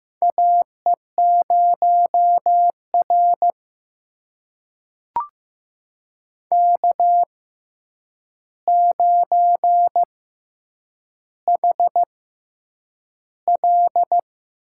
Morse code, heard as a keyed tone, about 15 wpm, spaced as for 8 wpm, 700 Hz. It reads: AE0R K9HL
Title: USA Call Signs - 4 Characters - 6x Character Spacing - 15wpm